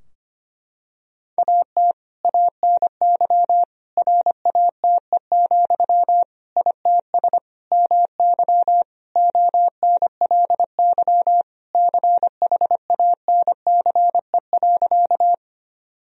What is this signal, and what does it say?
Morse code, heard as a keyed tone, about 25 wpm, 700 Hz. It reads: AT ANY RATE, ITS MY ONLY CHANCE.
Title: Sentences from Top 1000 Words - Review Part 4 - 25wpm